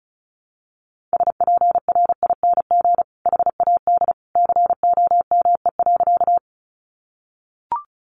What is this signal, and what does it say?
Morse code, heard as a keyed tone, about 34 wpm, 700 Hz. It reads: SPRING HAD COME.
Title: Sentences from Top 1200 Words 35wpm